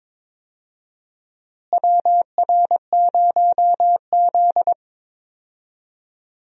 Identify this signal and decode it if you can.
Morse code, heard as a keyed tone, about 22 wpm, 700 Hz. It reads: WR0Z
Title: USA Call Signs - 4 Characters - Repeated 2x - 22wpm